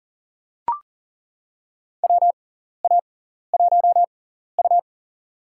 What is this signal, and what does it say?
Morse code, heard as a keyed tone, about 40 wpm, spaced as for 14 wpm, 700 Hz. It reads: WA1U
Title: USA Call Signs - 4 Characters - 6x Character Spacing - 40wpm